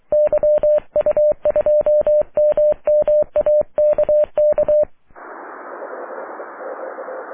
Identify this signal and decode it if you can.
Morse code, heard as a keyed tone, about 23 wpm, 600 Hz. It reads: YU2MMA KK
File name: wp / la8aja yu2mma aurora 05 aug 2011